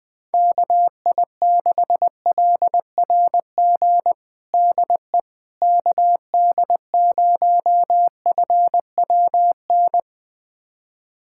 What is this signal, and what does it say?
Morse code, heard as a keyed tone, about 20 wpm, 700 Hz. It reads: KI6LRG DE KD0FWN
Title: USA Callsign DE Callsign - 6 Characters - 20wpm